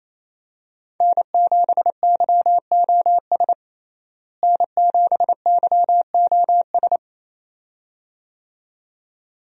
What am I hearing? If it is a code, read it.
Morse code, heard as a keyed tone, about 28 wpm, 700 Hz. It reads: N7YOS N7YOS
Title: USA Call Signs - 5 Characters - Repeated 2x - 28wpm